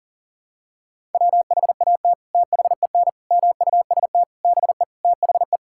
Morse code, heard as a keyed tone, about 40 wpm, 700 Hz. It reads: WHAT THEN MUST BE THE
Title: Sentences from Top 1000 Words - Review Part 5 - 40wpm